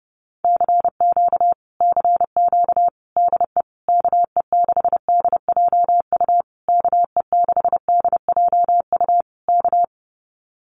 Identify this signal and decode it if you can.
Morse code, heard as a keyed tone, about 32 wpm, 700 Hz. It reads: CQ CQ DE KE6DJU KE6DJU K
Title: Common QSO Phrases Encore 30wpm